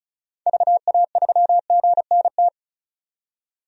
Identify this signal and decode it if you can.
Morse code, heard as a keyed tone, about 34 wpm, 700 Hz. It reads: VA3GNT
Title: DXCC-100 Call Signs - 6 Characters - 35wpm